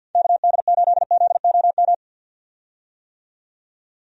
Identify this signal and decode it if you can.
Morse code, heard as a keyed tone, about 51 wpm, 700 Hz. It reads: KD8ZOM
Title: USA Call Signs - 6 Characters - 50wpm